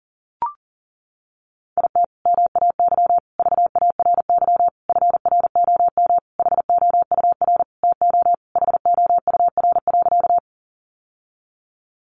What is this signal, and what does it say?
Morse code, heard as a keyed tone, about 40 wpm, 700 Hz. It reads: IT MAY VARY FROM HOUR TO HOUR.
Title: Sentences from Top 1200 Words 40wpm